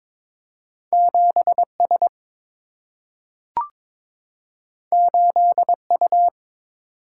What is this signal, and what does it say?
Morse code, heard as a keyed tone, about 22 wpm, 700 Hz. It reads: 7S 8U
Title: Single-Letter Single-Number 22wpm